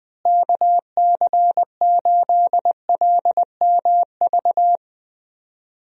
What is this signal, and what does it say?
Morse code, heard as a keyed tone, about 20 wpm, 700 Hz. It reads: KC8LMV
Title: USA Call Signs - 6 Characters - Repeated 2x - 20wpm